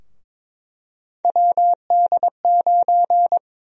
Morse code, heard as a keyed tone, about 22 wpm, 700 Hz. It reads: WD9
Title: USA Call Sign Prefixes - 22wpm